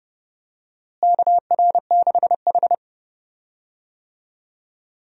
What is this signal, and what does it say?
Morse code, heard as a keyed tone, about 30 wpm, 700 Hz. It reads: KR6H
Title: USA Call Signs - 4 Characters - Repeated 2x - 30wpm